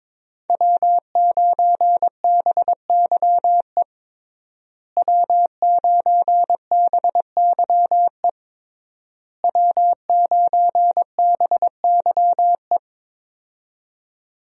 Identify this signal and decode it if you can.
Morse code, heard as a keyed tone, about 22 wpm, 700 Hz. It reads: W9BYE W9BYE W9BYE
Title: USA Call Signs - 5 Characters - Repeated 3x - 22wpm